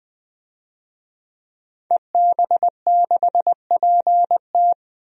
Morse code, heard as a keyed tone, about 20 wpm, 700 Hz. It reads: EB6PT
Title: DXCC-100 Call Signs - 5 Characters - 20wpm